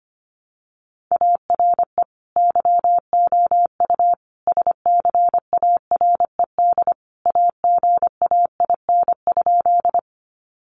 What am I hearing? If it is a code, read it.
Morse code, heard as a keyed tone, about 25 wpm, 700 Hz. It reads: ARE YOU SCARED AGAIN?
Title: Sentences from Top 700 Words 25wpm